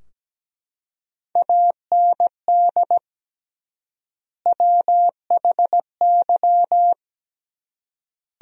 Morse code, heard as a keyed tone, about 17 wpm, 700 Hz. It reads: AND WHY
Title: 3x Word-Spacing - Sentences from Top 200 Words - 17wpm